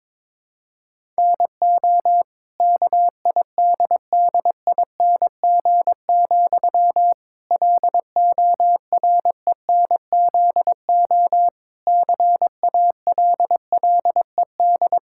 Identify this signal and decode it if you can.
Morse code, heard as a keyed tone, about 22 wpm, 700 Hz. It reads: NO KIDDING, LORENZO CALLED
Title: Pangram Challenge v2 - 22wpm